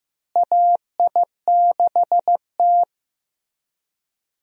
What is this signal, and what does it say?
Morse code, heard as a keyed tone, about 15 wpm, 700 Hz. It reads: AI6T
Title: USA Call Signs - 4 Characters - Repeated 3x - 15wpm